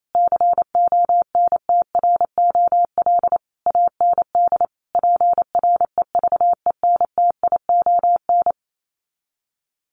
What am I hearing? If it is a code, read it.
Morse code, heard as a keyed tone, about 28 wpm, 700 Hz. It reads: CONTROL AND PREVENTION
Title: Sets of 5 Words Encore2 28wpm